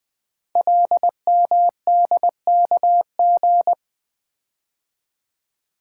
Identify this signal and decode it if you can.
Morse code, heard as a keyed tone, about 20 wpm, 700 Hz. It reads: LMDKG
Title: Code Groups - 5 Letters - 20wpm